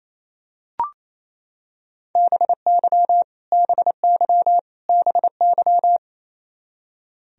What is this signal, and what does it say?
Morse code, heard as a keyed tone, about 27 wpm, 700 Hz. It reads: BY BY BY
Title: Top 100 Words Repeated 28wpm